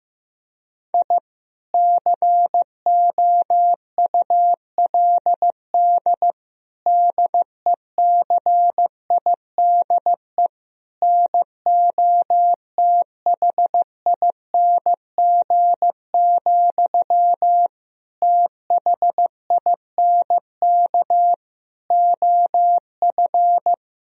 Morse code, heard as a keyed tone, about 15 wpm, 700 Hz. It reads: I COULD DECIDE NOTHING, THINK OF
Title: Sentences from Top 1000 Words - Review Part 3 - 15wpm